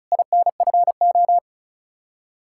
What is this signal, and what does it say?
Morse code, heard as a keyed tone, about 46 wpm, 700 Hz. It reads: INFO